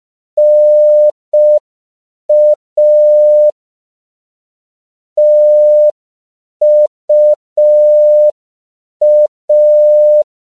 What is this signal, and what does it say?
Morse code, heard as a keyed tone, about 5 wpm, 600 Hz. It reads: NA TUA
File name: CW / 5PPM